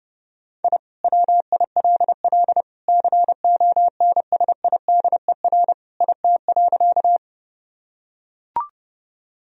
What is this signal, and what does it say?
Morse code, heard as a keyed tone, about 30 wpm, 700 Hz. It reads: I WILL CONSIDER IT.
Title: Sentences from Top 1000 Words - Review Part 4 - 30wpm